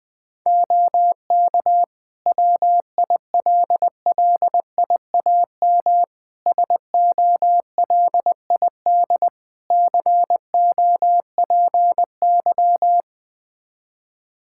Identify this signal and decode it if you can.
Morse code, heard as a keyed tone, about 20 wpm, 700 Hz. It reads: OK WILLIAM SOLID COPY
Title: Practice QSO - Exchange 3 - More Info - 20wpm